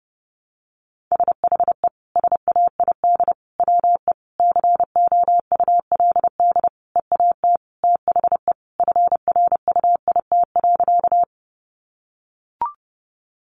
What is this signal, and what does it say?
Morse code, heard as a keyed tone, about 30 wpm, 700 Hz. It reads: SHE SAID WE COULD EAT THE FRUIT.
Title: Sentences from Top 1300 Words 30wpm